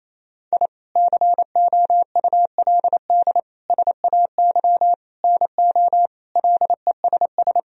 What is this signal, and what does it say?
Morse code, heard as a keyed tone, about 28 wpm, 700 Hz. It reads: I COULD SAY NO LESS
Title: Sentences from Top 1000 Words - Review Part 2 - 28wpm